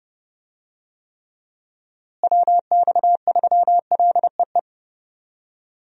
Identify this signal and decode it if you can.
Morse code, heard as a keyed tone, about 30 wpm, 700 Hz. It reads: WX3LEE
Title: USA Call Signs - 6 Characters - Repeated 2x - 30wpm